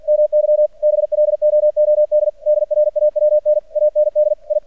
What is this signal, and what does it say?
Morse code, heard as a keyed tone, about 48 wpm, 600 Hz. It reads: MY BLOOD DRAWN AND I
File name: archived / CRSnet